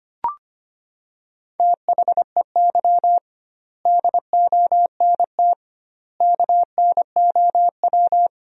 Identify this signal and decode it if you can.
Morse code, heard as a keyed tone, about 25 wpm, 700 Hz. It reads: THEY DONT KNOW
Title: 2x Word-Spacing - Sentences from Top 300 Words - 25wpm